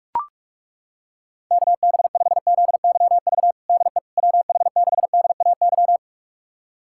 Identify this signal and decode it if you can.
Morse code, heard as a keyed tone, about 41 wpm, 700 Hz. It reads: KB5ZYV DE WH6DAY
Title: USA Callsign DE Callsign - 6 Characters - 45wpm